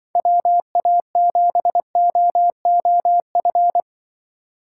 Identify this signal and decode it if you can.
Morse code, heard as a keyed tone, about 23 wpm, 700 Hz. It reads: WA7OOF
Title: Speed Racing - USA Call Signs 20wpm